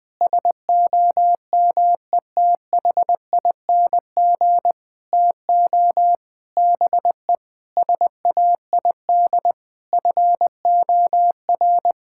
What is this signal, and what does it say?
Morse code, heard as a keyed tone, about 20 wpm, 700 Hz. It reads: SOMETHING TO BE SAID FOR
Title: Sentences with 7-Word Phrases - Spoken Beginning - 20wpm